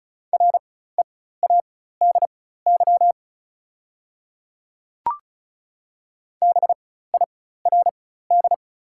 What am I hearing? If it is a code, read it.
Morse code, heard as a keyed tone, about 34 wpm, spaced as for 16 wpm, 690 Hz. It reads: READY BIRD
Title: 4x Character Spacing - Top 500 Words - 35wpm